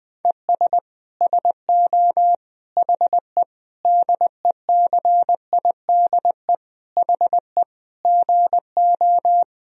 Morse code, heard as a keyed tone, about 21 wpm, 700 Hz. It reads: ES SO HE DECIDE HE GO
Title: Sentences from Top 500 Words and Abbreviations 20wpm